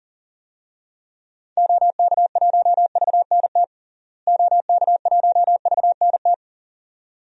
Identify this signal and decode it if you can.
Morse code, heard as a keyed tone, about 37 wpm, 700 Hz. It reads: OK1VNT OK1VNT
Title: DXCC-100 Call Signs - 6 Characters - Repeated 2x - 40wpm